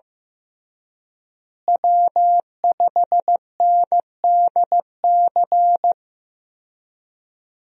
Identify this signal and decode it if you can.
Morse code, heard as a keyed tone, about 15 wpm, 700 Hz. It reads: W5NDC
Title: USA Call Signs - 5 Characters - Repeated 3x - 15wpm